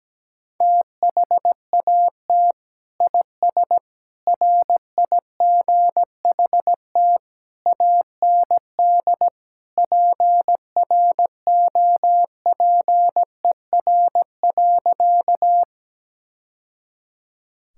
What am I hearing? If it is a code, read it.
Morse code, heard as a keyed tone, about 17 wpm, 700 Hz. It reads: THAT IS RIGHT AND PROPER.
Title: Sentences from Top 1000 Words - Review Part 5 - 17wpm